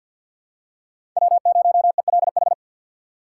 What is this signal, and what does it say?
Morse code, heard as a keyed tone, about 47 wpm, 700 Hz. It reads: W0ELH
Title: USA Call Signs - 5 Characters - 2x Character Spacing - 50wpm